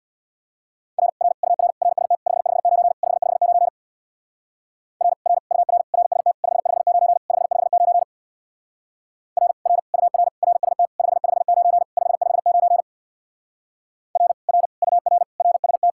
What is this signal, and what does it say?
Morse code, heard as a keyed tone, about 66 wpm, 700 Hz. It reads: R R UR RST 559 559 R R UR RST 559 559 R R UR RST 559 559 R R UR RST